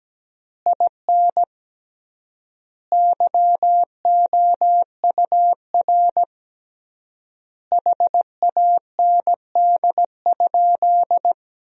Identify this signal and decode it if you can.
Morse code, heard as a keyed tone, about 17 wpm, 700 Hz. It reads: IN YOUR HAND?
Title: 3x Word-Spacing - Sentences from Top 200 Words - 17wpm